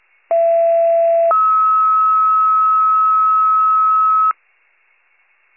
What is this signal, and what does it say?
Two-tone page, heard as a one-second tone, then a three-second tone, about 670 then 1290 Hz. It is heard from 0.3 to 4.3 s.